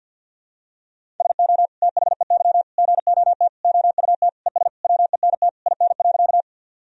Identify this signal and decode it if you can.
Morse code, heard as a keyed tone, about 49 wpm, 700 Hz. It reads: SO THEY GOT OUT ES WENT IN.
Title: Sentences from Top 300 Words and Abbreviations 50wpm